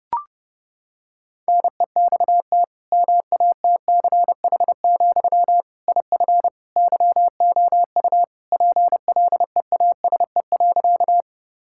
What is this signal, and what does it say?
Morse code, heard as a keyed tone, about 32 wpm, 700 Hz. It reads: NEXT MATCH, IF YOU PLEASE.
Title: Sentences from Top 1600 Words 30wpm